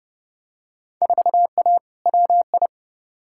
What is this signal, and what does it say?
Morse code, heard as a keyed tone, about 31 wpm, 700 Hz. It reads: 4A WI
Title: Field Day Exchanges - 30wpm